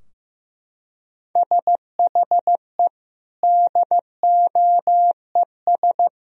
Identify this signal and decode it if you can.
Morse code, heard as a keyed tone, about 15 wpm, 700 Hz. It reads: SHE DOES
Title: Sentences from Top 1000 Words - Review Part 3 - 15wpm